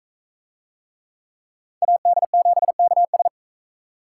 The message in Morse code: AD7KS

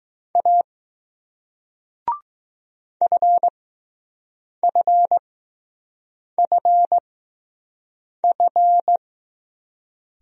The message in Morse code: A FFFF

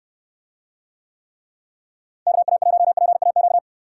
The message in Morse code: KA1PAQ

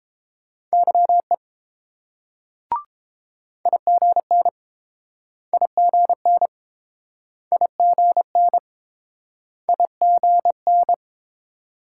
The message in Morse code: YE IGN IGN IGN IGN

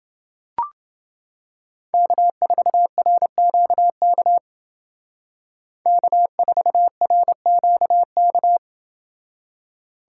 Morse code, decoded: K4RQK K4RQK

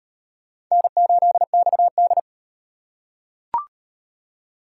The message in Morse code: N8XD

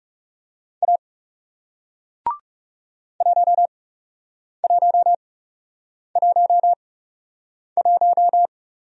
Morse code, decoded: A 1111

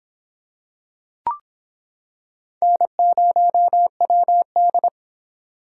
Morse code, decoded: N0WD

